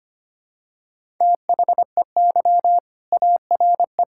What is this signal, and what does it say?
Morse code, heard as a keyed tone, about 25 wpm, 700 Hz. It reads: THEY ARE
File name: practice / 25wpm / Sentences with 2-Word Phrases - 25wpm